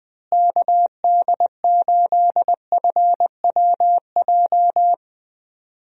Morse code, heard as a keyed tone, about 21 wpm, 700 Hz. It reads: KD8FWJ
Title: USA Call Signs - 6 Characters - Repeated 3x - 20wpm